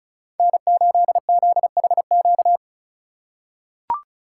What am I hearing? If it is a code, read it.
Morse code, heard as a keyed tone, about 34 wpm, 700 Hz. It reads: N8ZHQ
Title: USA Call Signs - 2x Character Spacing - 35wpm